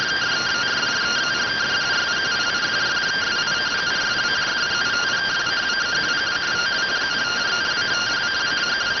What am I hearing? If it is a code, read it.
Two-tone teleprinter signal, about 170 Hz shift.